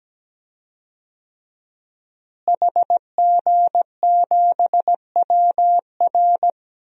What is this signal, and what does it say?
Morse code, heard as a keyed tone, about 17 wpm, 700 Hz. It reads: HG7WR
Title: DXCC-100 Call Signs - 5 Characters - Repeated 2x - 17wpm